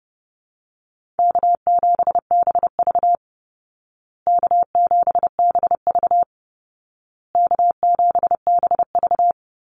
Morse code, heard as a keyed tone, about 28 wpm, 700 Hz. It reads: K7BV K7BV K7BV
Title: Easy Call Signs Repeated 3x - 30wpm